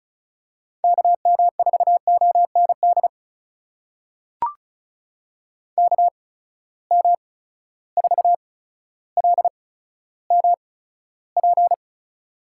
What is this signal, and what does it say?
Morse code, heard as a keyed tone, about 36 wpm, spaced as for 15 wpm, 700 Hz. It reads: KM4OND KM4LMP
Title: USA Call Signs - 6 Characters - 8x Character Spacing - 35wpm